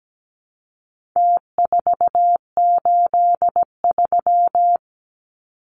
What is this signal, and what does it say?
Morse code, heard as a keyed tone, about 17 wpm, 700 Hz. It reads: T483
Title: Numbers - Contest Style - 4-Digit 17wpm